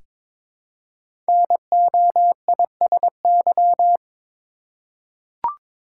Morse code, heard as a keyed tone, about 23 wpm, 700 Hz. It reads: NOISY
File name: 5 Letter Words - First 3 Letters Given - 22wpm